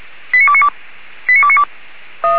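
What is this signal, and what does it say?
Five-tone selective call (ZVEI), heard as quick runs of five tones, tones of about 70 ms. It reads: 89292
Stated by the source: ZVEI five-tone selective call 89293 received by SDR, from the Feuerwehr Orlamünde fire brigade audio server.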